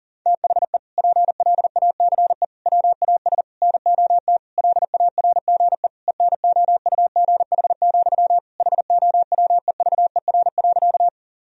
Morse code, heard as a keyed tone, about 40 wpm, 700 Hz. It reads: THE PLACE WAS NOT LARGE ENOUGH, HOWEVER.
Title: Sentences from Top 1000 Words - Review Part 1 - 40wpm